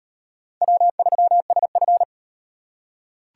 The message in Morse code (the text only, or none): W3SF